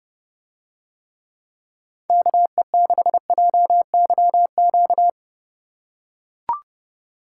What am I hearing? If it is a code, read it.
Morse code, heard as a keyed tone, about 30 wpm, 700 Hz. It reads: KE6JYQ